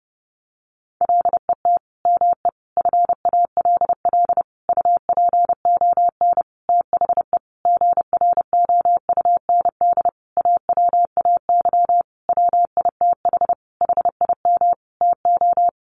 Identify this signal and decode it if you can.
Morse code, heard as a keyed tone, about 30 wpm, 700 Hz. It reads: LET ME FALL UPON THE GROUND AWAY WITH HIM TO
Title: Sentences from Top 1100 Words 30wpm